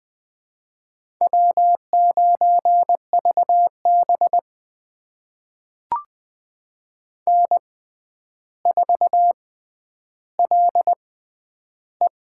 Morse code, heard as a keyed tone, about 20 wpm, spaced as for 10 wpm, 700 Hz. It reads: W9VB N4LE
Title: USA Call Signs - 4 Characters - 6x Character Spacing - 20wpm